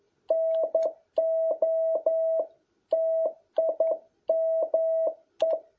Morse code, heard as a keyed tone, about 11 wpm, 650 Hz. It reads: NO TIME
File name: jh8cbh / 2021cw